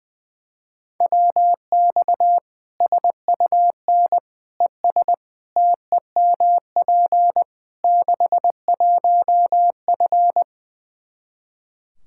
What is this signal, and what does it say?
Morse code, heard as a keyed tone, about 21 wpm, 700 Hz. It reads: WX SUN ES TEMP 61F
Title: Practice QSO - Exchange 2 - Info - 20wpm